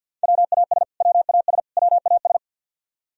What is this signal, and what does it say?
Morse code, heard as a keyed tone, about 53 wpm, 700 Hz. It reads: WAS WAS WAS
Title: Top 100 Words Repeated 50wpm